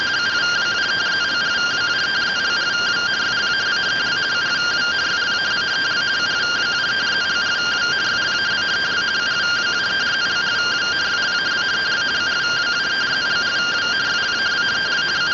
Two-tone teleprinter signal, about 170 Hz shift.